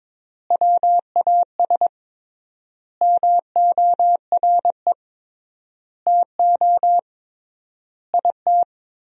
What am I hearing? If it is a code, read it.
Morse code, heard as a keyed tone, about 22 wpm, 700 Hz. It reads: WAS MORE TO IT
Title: 3x Word-Spacing - Sentences from Top 100 Words - 22wpm